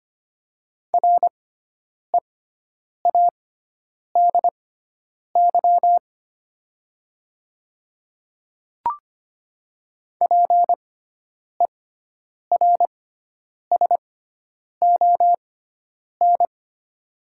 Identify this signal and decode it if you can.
Morse code, heard as a keyed tone, about 25 wpm, spaced as for 9 wpm, 700 Hz. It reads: READY PERSON
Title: 6x Character Spacing - Top 500 Words - 25wpm